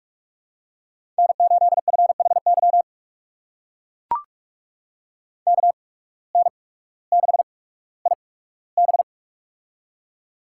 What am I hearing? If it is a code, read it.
Morse code, heard as a keyed tone, about 44 wpm, 690 Hz. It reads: N 8 F H Y K N 6 I B
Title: USA Call Signs - 8x Character Spacing - 45wpm